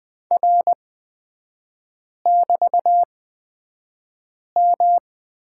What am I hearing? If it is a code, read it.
Morse code, heard as a keyed tone, about 20 wpm, spaced as for 5 wpm, 690 Hz. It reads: R<BT>M
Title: 229 Lesson 37e - Letter x - Three Characters 20wpm